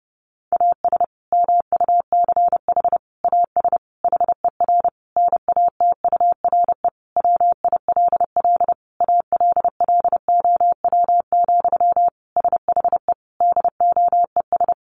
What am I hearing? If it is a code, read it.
Morse code, heard as a keyed tone, about 30 wpm, 700 Hz. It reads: AS MUCH AS HER NATURE WILL ALLOW, SHE DOES
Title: Sentences from Top 700 Words 30wpm